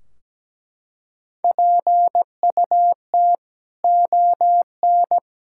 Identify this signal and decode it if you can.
Morse code, heard as a keyed tone, about 17 wpm, 700 Hz. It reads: PUT ON
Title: Sentences from Top 1600 Words 17wpm